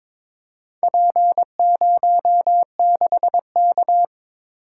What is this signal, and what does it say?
Morse code, heard as a keyed tone, about 22 wpm, 700 Hz. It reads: P06K
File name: practice / 22wpm / Four Letter-Number Combinations - 22wpm